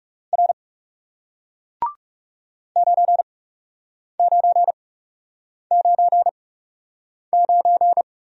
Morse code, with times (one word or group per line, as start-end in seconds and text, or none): 0.3-0.5 R
2.8-8.0 9999